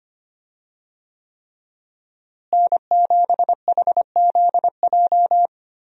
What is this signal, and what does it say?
Morse code, heard as a keyed tone, about 25 wpm, 700 Hz. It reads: N7HZJ
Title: USA Call Signs - 5 Characters - Repeated 3x - 25wpm